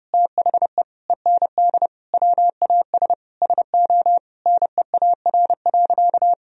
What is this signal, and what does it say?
Morse code, heard as a keyed tone, about 29 wpm, 700 Hz. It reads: THE END WAS SO NEAR.
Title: Sentences from Top 1000 Words - Review Part 1 - 30wpm